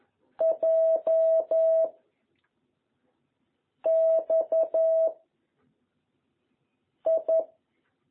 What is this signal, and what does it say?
Morse code, heard as a keyed tone, about 11 wpm, 660 Hz. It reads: J X I
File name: jh8cbh / 2021cw